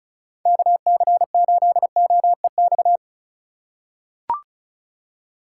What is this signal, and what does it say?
Morse code, heard as a keyed tone, about 36 wpm, 700 Hz. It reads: KC8OEX
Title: USA Call Signs - 2x Character Spacing - 35wpm